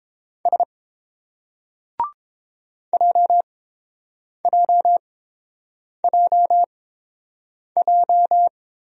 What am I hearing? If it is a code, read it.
Morse code, heard as a keyed tone, about 29 wpm, spaced as for 7 wpm, 700 Hz. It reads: S JJJJ